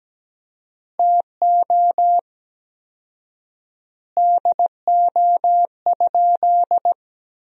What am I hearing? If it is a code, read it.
Morse code, heard as a keyed tone, about 17 wpm, 700 Hz. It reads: TO DO?
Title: 4x Word-Spacing - Sentences from Top 400 Words - 17wpm